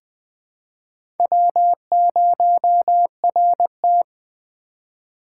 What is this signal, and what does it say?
Morse code, heard as a keyed tone, about 21 wpm, 700 Hz. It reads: W0RT
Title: USA Call Signs - 4 Characters - 2x Character Spacing - 20wpm